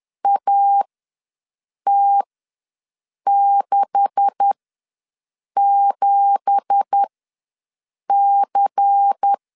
Morse code, heard as a keyed tone, about 10 wpm, spaced as for 6 wpm, 800 Hz. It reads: AT67C
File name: LEZIONI / 9 2867 ws 10wpm 4215